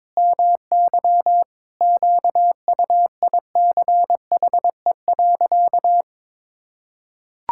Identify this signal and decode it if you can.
Morse code, heard as a keyed tone, about 22 wpm, 700 Hz. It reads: MY QUICHE.
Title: Pangram Challenge v2 - 22wpm